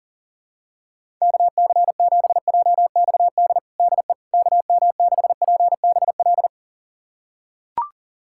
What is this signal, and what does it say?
Morse code, heard as a keyed tone, about 40 wpm, 700 Hz. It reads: KC7JXD DE KM6PBL
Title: USA Callsign DE Callsign - 6 Characters - 40wpm